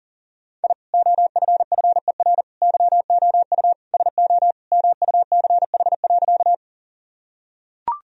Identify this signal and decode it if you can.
Morse code, heard as a keyed tone, about 40 wpm, 700 Hz. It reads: I OFFER YOU SO MUCH.
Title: Sentences from Top 1000 Words - Review Part 4 - 40wpm